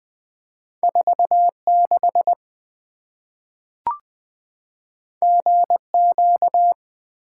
Morse code, heard as a keyed tone, about 20 wpm, 700 Hz. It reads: 46 GQ
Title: Two Letter-Number Combinations - 20wpm